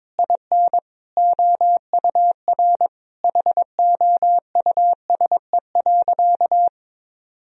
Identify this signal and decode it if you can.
Morse code, heard as a keyed tone, about 22 wpm, 700 Hz. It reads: IN OUR HOUSE.
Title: Sentences from Top 1000 Words - Review Part 1 - 22wpm